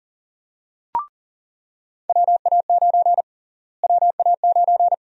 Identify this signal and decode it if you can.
Morse code, heard as a keyed tone, about 40 wpm, 700 Hz. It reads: WA9 WA9
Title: USA Call Sign Prefixes - Repeated 2x - 40wpm